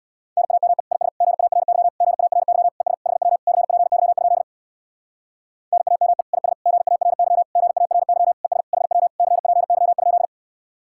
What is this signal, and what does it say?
Morse code, heard as a keyed tone, about 61 wpm, 700 Hz. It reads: NAME IS GARY GARY ES HW CPY? NAME IS GARY GARY ES HW CPY?
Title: Speed-Racing Common QSO Phrases - 50wpm